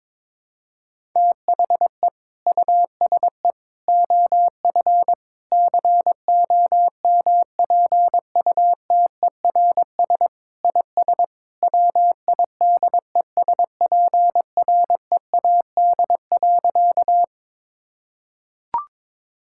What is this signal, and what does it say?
Morse code, heard as a keyed tone, about 22 wpm, 700 Hz. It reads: THE USE OF COMPUTERS IS WIDESPREAD.
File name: practice / 22wpm / Sentences with 3-Word Phrases - 22wpm